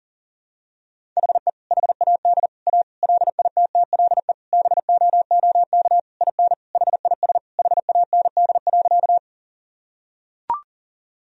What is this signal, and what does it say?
Morse code, heard as a keyed tone, about 40 wpm, 700 Hz. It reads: HE HAD A LITTLE BOOK IN HIS HAND.
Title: Sentences from Top 1000 Words - Review Part 1 - 40wpm